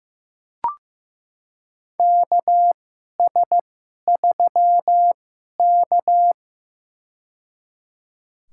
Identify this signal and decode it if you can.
Morse code, heard as a keyed tone, about 15 wpm, spaced as for 11 wpm, 700 Hz. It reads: KS3K
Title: USA Call Signs - 2x Character Spacing - 15wpm